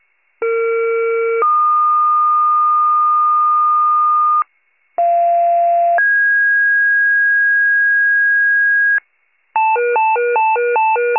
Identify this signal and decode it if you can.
Two-tone page, heard as a one-second tone, then a three-second tone, about 460 then 1220 Hz, about 690 then 1650 Hz, with noise. It is heard from 0.4 to 9.0 s.